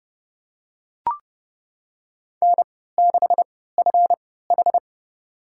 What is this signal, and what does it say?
Morse code, heard as a keyed tone, about 28 wpm, spaced as for 17 wpm, 700 Hz. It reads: N6FH
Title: USA Call Signs - 4 Characters - 3x Character Spacing - 30wpm